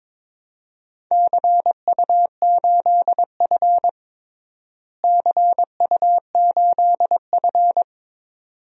Morse code, heard as a keyed tone, about 22 wpm, 700 Hz. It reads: CU8F CU8F
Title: DXCC-100 Call Signs - 4 Characters - Repeated 2x - 22wpm